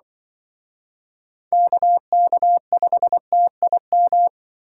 Morse code, heard as a keyed tone, about 25 wpm, 700 Hz. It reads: KK5TIM
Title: Speed Racing - USA 2x3 Call Signs 20wpm